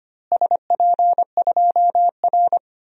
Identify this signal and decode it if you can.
Morse code, heard as a keyed tone, about 25 wpm, 700 Hz. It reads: SP2R
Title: DXCC-100 Call Signs - 4 Characters - Repeated 3x - 25wpm